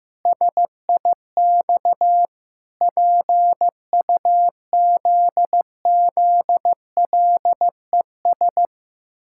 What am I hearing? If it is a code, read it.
Morse code, heard as a keyed tone, about 16 wpm, 700 Hz. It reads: SIX PUZZLES
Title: Pangram Challenge v2 - 15wpm